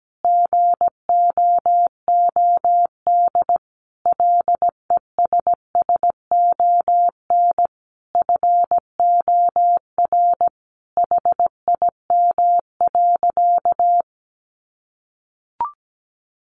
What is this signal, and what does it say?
Morse code, heard as a keyed tone, about 17 wpm, 700 Hz. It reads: GOOD LESSON FOR HIM.
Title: Sentences from Top 1200 Words 17wpm